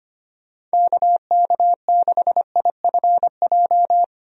Morse code, heard as a keyed tone, about 25 wpm, 700 Hz. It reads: KK6IFJ
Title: USA Call Signs - 6 Characters - 25wpm